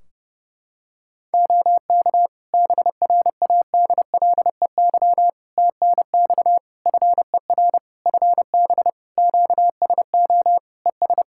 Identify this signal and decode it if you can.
Morse code, heard as a keyed tone, about 30 wpm, 700 Hz. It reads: OK BRADLEY TNX FER FB QSO ES
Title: Practice QSO - Exchange 4 - Ending - 30wpm